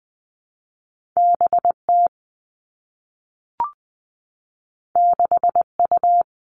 Morse code, heard as a keyed tone, about 19 wpm, 700 Hz. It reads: B T 6 U
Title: Two Challenging Letter-Number Combinations 20wpm